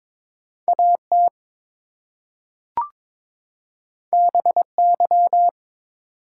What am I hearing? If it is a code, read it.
Morse code, heard as a keyed tone, about 22 wpm, 700 Hz. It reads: AT BY
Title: Two Letter Words - 22wpm